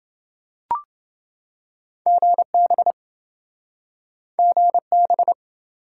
Morse code, heard as a keyed tone, about 28 wpm, 700 Hz. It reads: GB GB